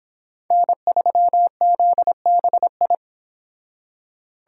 Morse code, heard as a keyed tone, about 29 wpm, 700 Hz. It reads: N3ZBI